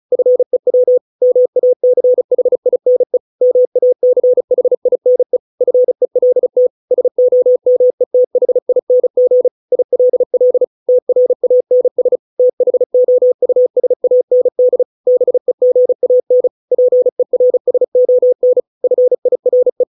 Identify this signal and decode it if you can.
Morse code, heard as a keyed tone, about 34 wpm, 500 Hz. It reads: FEW MACHINE MACHINE FELT SOMETHING ILL TRANS THOUSAND BEGAN PERSON FIRE